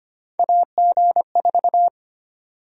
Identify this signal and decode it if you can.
Morse code, heard as a keyed tone, about 25 wpm, 700 Hz. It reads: AG4
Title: USA Call Sign Prefixes - 25wpm